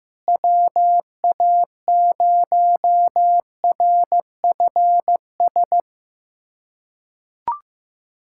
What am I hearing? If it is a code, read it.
Morse code, heard as a keyed tone, about 15 wpm, 700 Hz. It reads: WA0RFS
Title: USA Callsign DE Callsign - 6 Characters - 15wpm